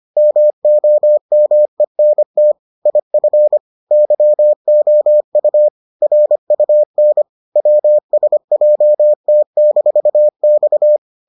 Morse code, heard as a keyed tone, about 25 wpm, 600 Hz. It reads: MOMENT IF YOU RUN WSJT-X